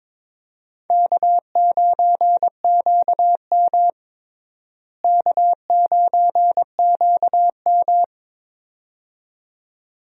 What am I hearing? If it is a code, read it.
Morse code, heard as a keyed tone, about 22 wpm, 700 Hz. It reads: K9QM K9QM
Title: USA Call Signs - 4 Characters - Repeated 2x - 22wpm